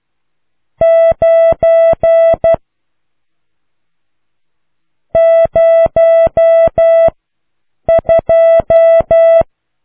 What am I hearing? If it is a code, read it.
Morse code, heard as a keyed tone, about 12 wpm, 660 Hz. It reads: 9 0 2